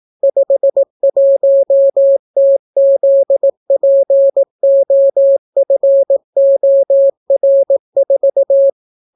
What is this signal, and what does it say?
Morse code, heard as a keyed tone, about 18 wpm, 550 Hz. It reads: 51TZPOFOR4